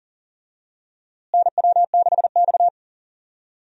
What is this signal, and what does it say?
Morse code, heard as a keyed tone, about 40 wpm, 700 Hz. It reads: NW6X
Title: USA Call Signs - 4 Characters - Repeated 2x - 40wpm